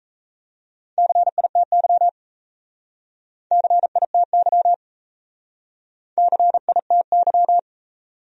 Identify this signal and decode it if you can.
Morse code, heard as a keyed tone, about 34 wpm, 700 Hz. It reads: CITY CITY CITY
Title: Speed-Racing Top 300 Words 28wpm